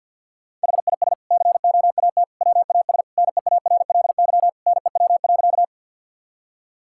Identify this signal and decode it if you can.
Morse code, heard as a keyed tone, about 50 wpm, 700 Hz. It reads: HIS COAT WAS NEARLY NEW.
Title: Sentences from Top 1600 Words 50wpm